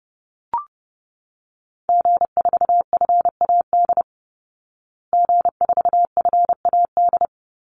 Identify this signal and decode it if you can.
Morse code, heard as a keyed tone, about 30 wpm, 700 Hz. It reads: G4FAD G4FAD
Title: Call Signs Repeated 3x - 30wpm